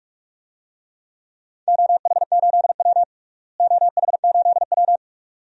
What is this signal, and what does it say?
Morse code, heard as a keyed tone, about 46 wpm, 700 Hz. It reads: OH8W OH8W
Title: DXCC-100 Call Signs - 4 Characters - Repeated 2x - 45wpm